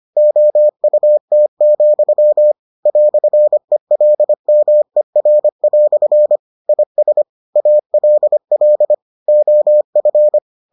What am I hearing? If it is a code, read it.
Morse code, heard as a keyed tone, about 25 wpm, 600 Hz. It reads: OUT, "ELMER" IS ALL OF